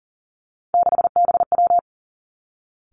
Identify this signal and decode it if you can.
Morse code, heard as a keyed tone, about 41 wpm, 690 Hz. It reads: 6BW
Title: Three Challenging Letter-Number Combinations 40wpm